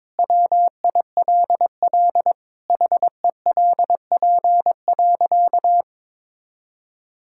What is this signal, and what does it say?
Morse code, heard as a keyed tone, about 22 wpm, 700 Hz. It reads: WILL HELP.
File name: practice / 22wpm / Sentences with 3-Word Phrases - 22wpm